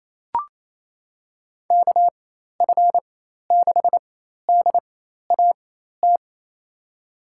Morse code, heard as a keyed tone, about 28 wpm, spaced as for 13 wpm, 700 Hz. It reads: KF6DAT
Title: USA Call Signs - 4x Character Spacing - 28wpm